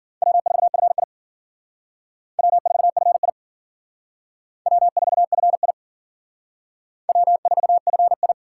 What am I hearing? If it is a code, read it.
Morse code, heard as a keyed tone, about 51 wpm, 700 Hz. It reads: W4FI W4FI W4FI W4FI